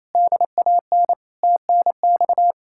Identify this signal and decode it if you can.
Morse code, heard as a keyed tone, about 38 wpm, 700 Hz. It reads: DAN TNX